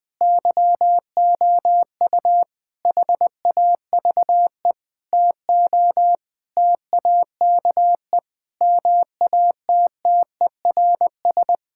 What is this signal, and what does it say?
Morse code, heard as a keyed tone, about 20 wpm, 700 Hz. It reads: YOU HAVE TO TAKE MATTERS
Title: Sentences with 7-Word Phrases - 20wpm